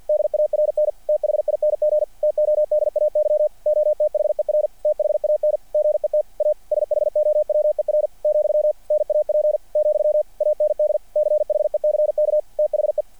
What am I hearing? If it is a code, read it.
Morse code, heard as a keyed tone, about 47 wpm, 600 Hz. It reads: DARN THING TODAY OTHER THAN GET A SHOWER , NAP , AND CHECK THE
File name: archived / CRSnet